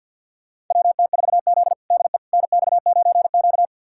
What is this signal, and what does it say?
Morse code, heard as a keyed tone, about 50 wpm, 700 Hz. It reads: WT4Z DE NX9Q
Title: USA Callsign DE Callsign - 4 Characters - 50wpm